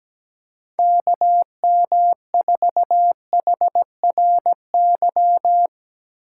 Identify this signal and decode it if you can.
Morse code, heard as a keyed tone, about 17 wpm, 700 Hz. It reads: KM4HRY
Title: USA Call Signs - 6 Characters - 2x Character Spacing - 17wpm